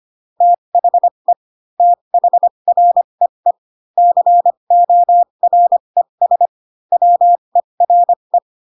Morse code, heard as a keyed tone, about 25 wpm, spaced as for 22 wpm, 700 Hz. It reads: THE THREE CORES WERE
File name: Pinocchio / Pinocchio ch7 18 25 ik1hge